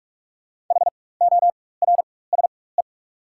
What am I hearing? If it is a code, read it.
Morse code, heard as a keyed tone, about 47 wpm, spaced as for 21 wpm, 690 Hz. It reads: HORSE
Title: 4x Character Spacing - Top 500 Words - 45wpm